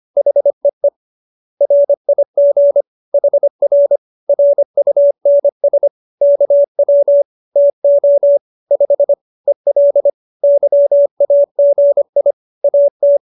Morse code, heard as a keyed tone, about 25 wpm, 570 Hz. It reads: HEE RIG HR RUNS KW TO 5 EL YAGI AT